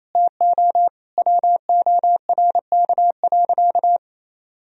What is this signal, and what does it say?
Morse code, heard as a keyed tone, about 28 wpm, 700 Hz. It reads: TO WORK.
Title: Sentences from Top 1000 Words - Review Part 3 - 28wpm